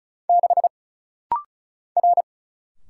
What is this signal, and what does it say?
Morse code, heard as a keyed tone, about 34 wpm, 700 Hz. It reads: 6 R
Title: Single Letter-Number Alt - Rapid-Fire 35wpm